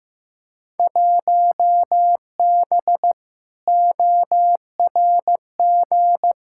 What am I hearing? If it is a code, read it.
Morse code, heard as a keyed tone, about 15 wpm, 700 Hz. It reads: 1B ORG
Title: Field Day Exchanges Encore - 15wpm